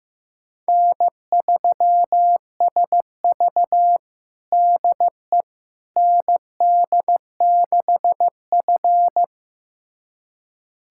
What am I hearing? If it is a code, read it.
Morse code, heard as a keyed tone, about 15 wpm, 700 Hz. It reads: N3SV DE ND6F
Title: USA Callsign DE Callsign - 4 Characters - 15wpm